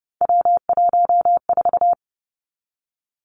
Morse code, heard as a keyed tone, about 30 wpm, 700 Hz. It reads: W14
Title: Three Challenging Letter-Number Combinations 30wpm